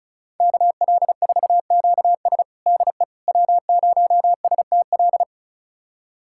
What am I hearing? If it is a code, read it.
Morse code, heard as a keyed tone, about 36 wpm, 700 Hz. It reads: KL4QS DE W0STL
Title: USA Callsign DE Callsign - 5 Characters - 35wpm